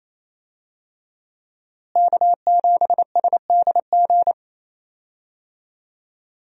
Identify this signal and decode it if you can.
Morse code, heard as a keyed tone, about 28 wpm, 700 Hz. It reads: K7SDG
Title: USA Call Signs - 5 Characters - Repeated 3x - 28wpm